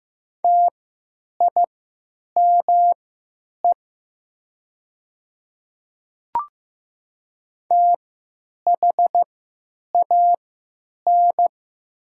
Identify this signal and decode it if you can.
Morse code, heard as a keyed tone, about 15 wpm, spaced as for 9 wpm, 700 Hz. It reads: TIME THAN
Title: 3x Character Spacing - Top 300 Words - 15wpm